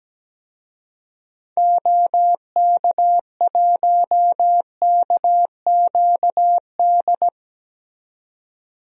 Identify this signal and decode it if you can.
Morse code, heard as a keyed tone, about 17 wpm, 700 Hz. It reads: OK1KQD
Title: DXCC-100 Call Signs - 6 Characters - 17wpm